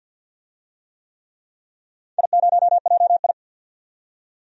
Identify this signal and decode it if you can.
Morse code, heard as a keyed tone, about 47 wpm, 700 Hz. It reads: I0JI
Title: DXCC-100 Call Signs - 4 Characters - 50wpm